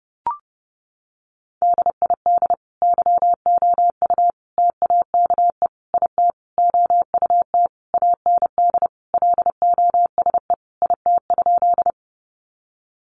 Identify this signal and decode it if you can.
Morse code, heard as a keyed tone, about 30 wpm, 700 Hz. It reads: DID YOU TAKE IT OUT AND LOSE IT?
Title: Sentences from Top 1100 Words 30wpm